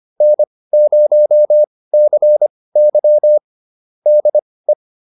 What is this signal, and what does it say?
Morse code, heard as a keyed tone, about 25 wpm, spaced as for 18 wpm, 600 Hz. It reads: N0CY DE